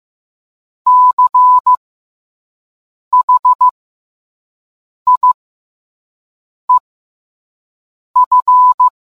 Morse code, heard as a keyed tone, about 15 wpm, spaced as for 5 wpm, 1000 Hz. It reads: CHIEF